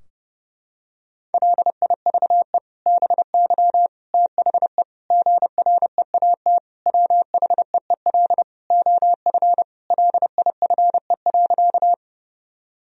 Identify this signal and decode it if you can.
Morse code, heard as a keyed tone, about 30 wpm, 700 Hz. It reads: LIVE BY THE GREAT WHEEL OF LIFE.
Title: Sentences from Top 1000 Words - Review Part 3 - 30wpm